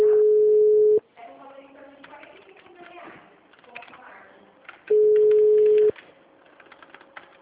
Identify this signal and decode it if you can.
Ringback tone (Europe, 425 Hz), with speech.